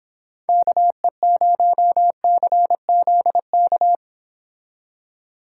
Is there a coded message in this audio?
Yes: Morse code from 0.5 to 4.0 s, heard as a keyed tone, about 26 wpm, 700 Hz. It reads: KE0CZK